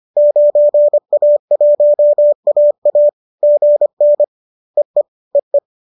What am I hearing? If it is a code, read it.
Morse code, heard as a keyed tone, about 25 wpm, 590 Hz. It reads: 9A1AA GN EE EE